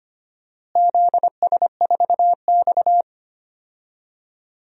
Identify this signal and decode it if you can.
Morse code, heard as a keyed tone, about 25 wpm, 700 Hz. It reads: ZS4X
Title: DXCC-100 Call Signs - 4 Characters - Repeated 2x - 25wpm